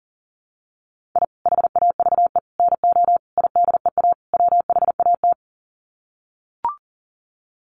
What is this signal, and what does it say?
Morse code, heard as a keyed tone, about 40 wpm, 700 Hz. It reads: I HAVE NO IDEA WHAT
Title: Sets of 5 Words Encore 40wpm